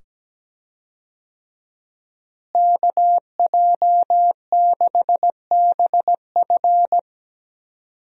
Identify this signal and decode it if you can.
Morse code, heard as a keyed tone, about 17 wpm, 700 Hz. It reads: KJ6BF
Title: USA Call Signs - 5 Characters - Repeated 2x - 17wpm